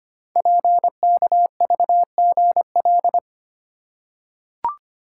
Morse code, heard as a keyed tone, about 25 wpm, 690 Hz. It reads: PKVGL
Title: Code Groups - 5 Letters - 25wpm